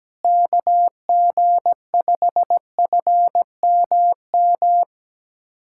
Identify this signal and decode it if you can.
Morse code, heard as a keyed tone, about 17 wpm, 700 Hz. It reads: KG5FMM
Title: USA Callsign DE Callsign - 6 Characters - 17wpm